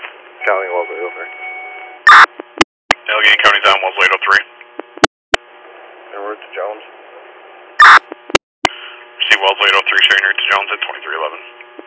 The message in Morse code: none

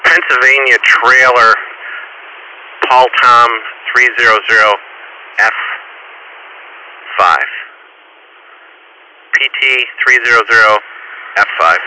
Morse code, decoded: none

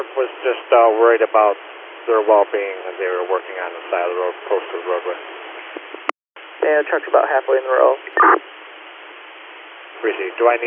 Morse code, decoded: none

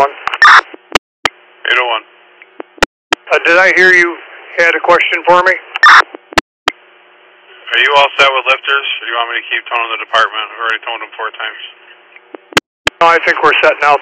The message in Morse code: none

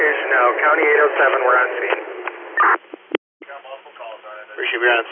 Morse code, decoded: none

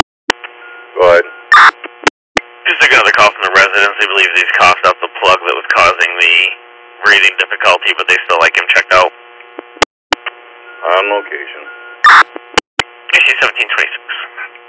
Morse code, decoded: none

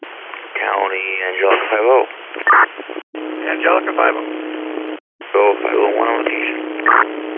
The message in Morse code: none

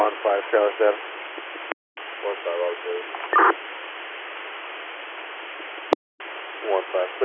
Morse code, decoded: none

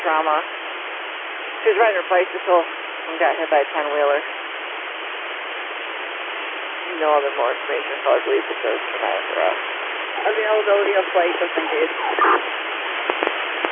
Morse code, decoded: none

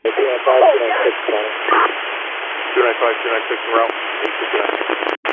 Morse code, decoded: none

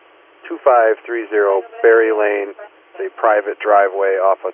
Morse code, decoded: none

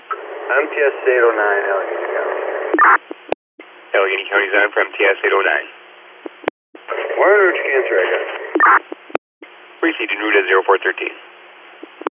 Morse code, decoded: none

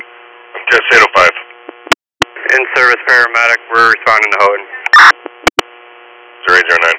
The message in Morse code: none

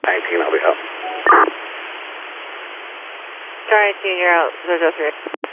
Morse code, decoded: none

